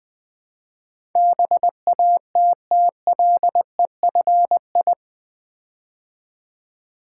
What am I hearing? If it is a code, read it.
Morse code, heard as a keyed tone, about 20 wpm, 700 Hz. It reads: BATTLEFI
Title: Build A Word - 4 Letters At A Time - 11-Letter Words - 20wpm